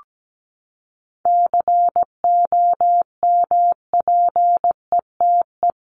Morse code, heard as a keyed tone, about 17 wpm, 700 Hz. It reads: COMPETE
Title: Top 1751-2000 Words 17wpm